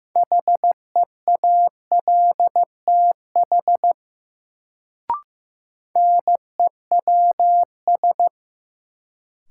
Morse code, heard as a keyed tone, about 15 wpm, 700 Hz. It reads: HEALTH NEWS
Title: Sets of 5 Words - New Word Drill - 15wpm